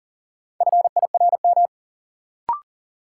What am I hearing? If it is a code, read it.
Morse code, heard as a keyed tone, about 40 wpm, 700 Hz. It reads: FIRM
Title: Sentences from Top 1400 Words - New Word Drill - 40wpm